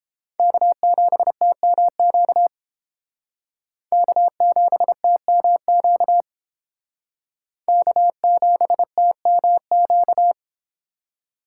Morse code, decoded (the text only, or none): K7TMQ K7TMQ K7TMQ